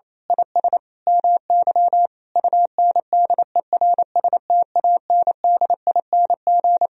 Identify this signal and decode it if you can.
Morse code, heard as a keyed tone, about 28 wpm, 700 Hz. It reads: IS MY UNDERSTANDING
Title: Sentences from Top 1000 Words - Review Part 3 - 28wpm